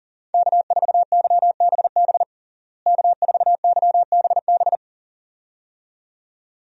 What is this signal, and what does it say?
Morse code, heard as a keyed tone, about 40 wpm, 700 Hz. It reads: K4YBB K4YBB
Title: USA Call Signs - 5 Characters - Repeated 2x - 40wpm